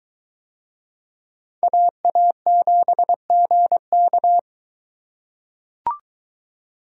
Morse code, heard as a keyed tone, about 22 wpm, 700 Hz. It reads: AA7GK